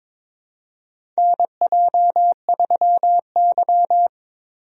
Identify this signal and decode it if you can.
Morse code, heard as a keyed tone, about 22 wpm, 700 Hz. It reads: NJ3Y
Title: USA Call Signs - 4 Characters - Repeated 3x - 22wpm